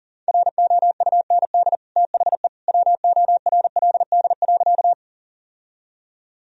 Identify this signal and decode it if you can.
Morse code, heard as a keyed tone, about 40 wpm, 700 Hz. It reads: ROUND THE WORLD.
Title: Sentences from Top 1000 Words - Review Part 1 - 40wpm